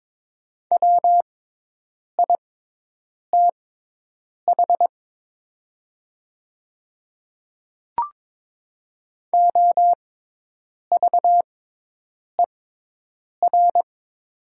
Morse code, heard as a keyed tone, about 22 wpm, spaced as for 8 wpm, 700 Hz. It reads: WITH OVER
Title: 6x Character Spacing - Top 100 Words - 22wpm